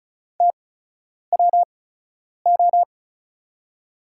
Morse code, heard as a keyed tone, about 36 wpm, spaced as for 10 wpm, 700 Hz. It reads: TWO
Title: 8x Character Spacing - Top 200 Words - 35wpm